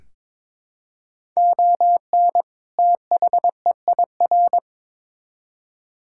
Morse code, heard as a keyed tone, about 21 wpm, 700 Hz. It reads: ON THEIR
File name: Build A Phrase - 2-Word Phrases - 22wpm